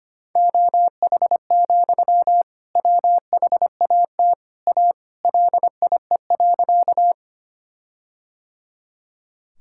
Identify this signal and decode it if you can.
Morse code, heard as a keyed tone, about 25 wpm, 700 Hz. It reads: OH, WHAT A LIE.
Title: Sentences from Top 1600 Words 25wpm